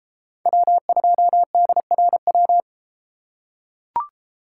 Morse code, heard as a keyed tone, about 33 wpm, 700 Hz. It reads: W2DRW